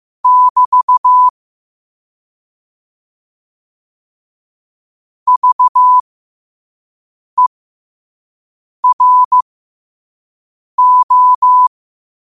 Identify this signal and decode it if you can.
Morse code, heard as a keyed tone, about 15 wpm, spaced as for 5 wpm, 1000 Hz. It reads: <BT> VERO
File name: qotdfiles / 5wpm